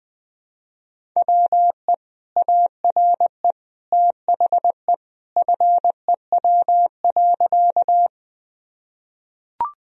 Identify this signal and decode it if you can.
Morse code, heard as a keyed tone, about 20 wpm, 700 Hz. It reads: WE ARE THE FEW.
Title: Sentences from Top 1000 Words - Review Part 1 - 20wpm